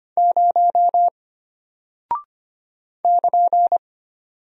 Morse code, heard as a keyed tone, about 25 wpm, 700 Hz. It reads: 0 <KN>
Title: Single Character (Letter:Number:Punctuation:Prosign) Alt 25wpm